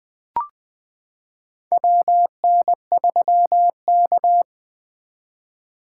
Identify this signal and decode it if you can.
Morse code, heard as a keyed tone, about 20 wpm, 700 Hz. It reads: WN3K
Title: K1USN SST QSO Practice - 20wpm